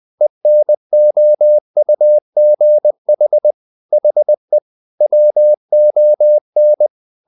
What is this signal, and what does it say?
Morse code, heard as a keyed tone, about 20 wpm, 600 Hz. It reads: ENOUGH HE WON
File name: the old man and the sea / the old man and the sea part 1 20wpm